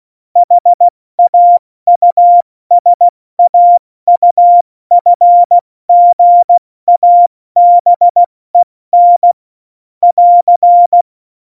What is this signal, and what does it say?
Morse code, heard as a keyed tone, about 16 wpm, 700 Hz. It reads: HAUSAUFGABEN <AR>
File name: Morsekurs / Witze / Witz 26. Februar 2019, 70 F80